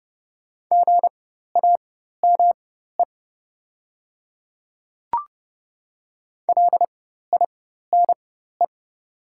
Morse code, heard as a keyed tone, about 30 wpm, spaced as for 14 wpm, 700 Hz. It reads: GAME LINE
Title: 4x Character Spacing - Top 500 Words - 30wpm